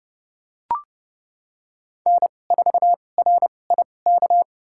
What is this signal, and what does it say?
Morse code, heard as a keyed tone, about 30 wpm, spaced as for 22 wpm, 700 Hz. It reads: N4RIK
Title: USA Call Signs - 5 Characters - 2x Character Spacing - 30wpm